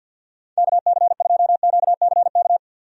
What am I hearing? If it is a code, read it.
Morse code, heard as a keyed tone, about 51 wpm, 700 Hz. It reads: KC2QCK
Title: USA Call Signs - 6 Characters - 50wpm